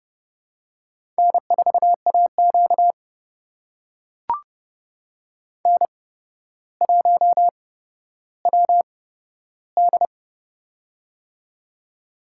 Morse code, heard as a keyed tone, about 30 wpm, spaced as for 13 wpm, 700 Hz. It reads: N4AQ N1WD
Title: USA Call Signs - 4 Characters - 8x Character Spacing - 30wpm